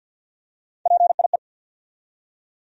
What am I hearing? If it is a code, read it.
Morse code, heard as a keyed tone, about 49 wpm, 700 Hz. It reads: PIE
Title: USA Call Sign Postfixes - 3 Letters 50wpm